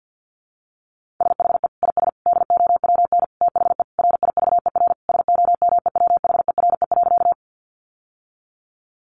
Morse code, heard as a keyed tone, about 50 wpm, 700 Hz. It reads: SHE IS DOWN THE RIVER SOMEWHERE.
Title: Sentences from Top 1400 Words 50wpm